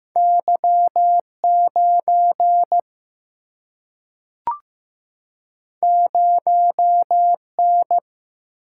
Morse code, heard as a keyed tone, about 15 wpm, 700 Hz. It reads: Y9 0N
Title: Single-Letter Single-Number 15wpm